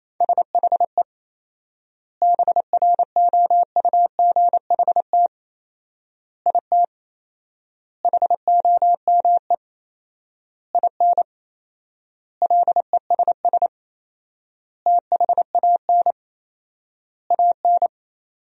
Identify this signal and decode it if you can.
Morse code, heard as a keyed tone, about 27 wpm, 700 Hz. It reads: SHE BROUGHT IT HOME IN LESS THAN AN
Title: 4x Word-Spacing - Sentences from Top 500 Words - 28wpm